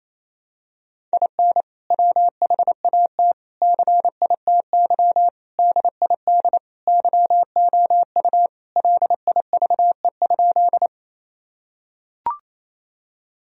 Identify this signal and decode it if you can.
Morse code, heard as a keyed tone, about 28 wpm, 700 Hz. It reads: IN WHAT CITY DID YOU LIVE?
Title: Sentences from Top 1000 Words - Review Part 1 - 28wpm